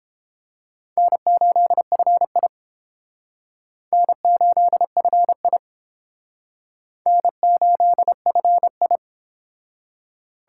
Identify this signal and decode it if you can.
Morse code, heard as a keyed tone, about 30 wpm, 700 Hz. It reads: N8FI N8FI N8FI